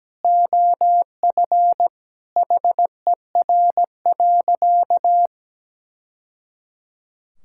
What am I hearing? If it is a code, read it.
Morse code, heard as a keyed tone, about 17 wpm, 700 Hz. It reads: OF HER.
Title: Sentences from Top 1000 Words - Review Part 5 - 17wpm